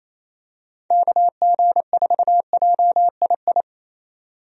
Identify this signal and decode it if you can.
Morse code, heard as a keyed tone, about 27 wpm, 700 Hz. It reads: KG4JII
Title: USA Call Signs - 6 Characters - 28wpm